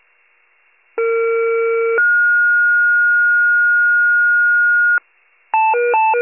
Two-tone page, heard as a one-second tone, then a three-second tone, about 460 then 1430 Hz. It is heard from 1.0 to 5.0 s.